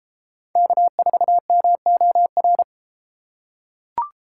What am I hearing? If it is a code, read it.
Morse code, heard as a keyed tone, about 32 wpm, 700 Hz. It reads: K4MOR